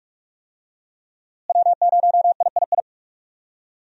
Morse code, heard as a keyed tone, about 43 wpm, 700 Hz. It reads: W0III
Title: USA Call Signs - 5 Characters - Repeated 3x - 45wpm